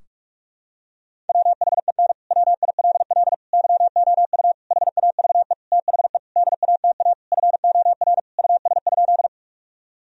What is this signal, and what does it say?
Morse code, heard as a keyed tone, about 46 wpm, 700 Hz. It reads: WHEN WILL YOU HAVE THE DATA FOR US?
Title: Sentences from Top 1000 Words - Review Part 3 - 45wpm